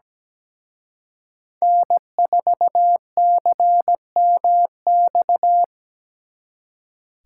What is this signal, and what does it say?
Morse code, heard as a keyed tone, about 20 wpm, 700 Hz. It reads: N4CMX
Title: Speed Racing - USA 1x3 Call Signs 17wpm